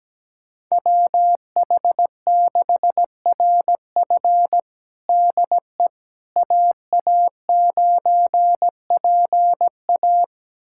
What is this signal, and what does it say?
Morse code, heard as a keyed tone, about 17 wpm, 700 Hz. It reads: WH6RF DE AA9PA
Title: USA Callsign DE Callsign - 5 Characters - 17wpm